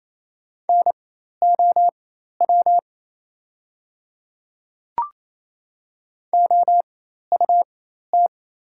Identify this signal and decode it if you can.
Morse code, heard as a keyed tone, about 28 wpm, spaced as for 13 wpm, 700 Hz. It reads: NOW OUT
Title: 4x Character Spacing - Top 100 Words - 28wpm